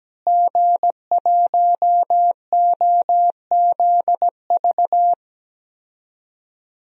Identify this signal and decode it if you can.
Morse code, heard as a keyed tone, about 17 wpm, 700 Hz. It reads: G1OZV
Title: DXCC-100 Call Signs - 5 Characters - Repeated 2x - 17wpm